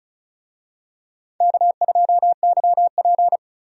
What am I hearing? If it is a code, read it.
Morse code, heard as a keyed tone, about 36 wpm, 700 Hz. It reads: K2YP
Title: USA Call Signs - 4 Characters - 35wpm